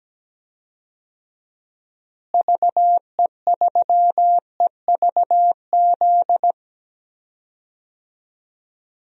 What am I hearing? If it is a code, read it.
Morse code, heard as a keyed tone, about 17 wpm, 700 Hz. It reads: VE3EVZ
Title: DXCC-100 Call Signs - 6 Characters - Repeated 3x - 17wpm